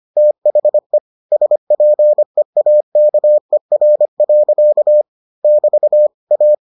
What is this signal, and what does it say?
Morse code, heard as a keyed tone, about 25 wpm, 600 Hz. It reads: THE SPEAKER. <BT> A